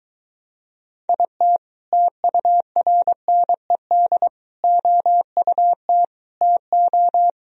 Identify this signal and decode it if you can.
Morse code, heard as a keyed tone, about 23 wpm, 700 Hz. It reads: IT TURNED OUT TO